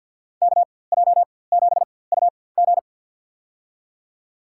Morse code, heard as a keyed tone, about 47 wpm, spaced as for 23 wpm, 700 Hz. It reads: KJ7UG
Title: USA Call Signs - 5 Characters - 4x Character Spacing - 50wpm